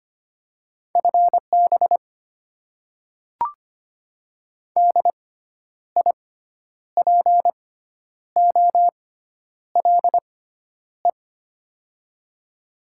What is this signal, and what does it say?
Morse code, heard as a keyed tone, about 25 wpm, spaced as for 10 wpm, 700 Hz. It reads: FB DIPOLE
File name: Top 90 QSO Elements - 6x Character Spacing - 25wpm